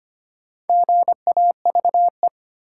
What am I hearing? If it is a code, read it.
Morse code, heard as a keyed tone, about 25 wpm, 700 Hz. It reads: GAVE
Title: Sentences from Top 500 Words - New Word Drill - 25wpm